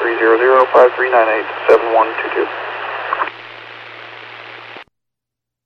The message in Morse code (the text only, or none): none